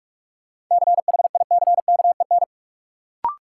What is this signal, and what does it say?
Morse code, heard as a keyed tone, about 46 wpm, 700 Hz. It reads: CHICKEN
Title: Sentences from Top 1000 Words - Review Part 5 - New Word Drill - 45wpm